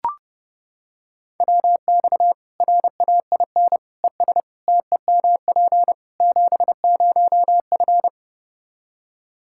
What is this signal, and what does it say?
Morse code, heard as a keyed tone, about 30 wpm, 700 Hz. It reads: WX RAIN ES TEMP 70F
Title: Practice QSO - Exchange 2 - Info - 30wpm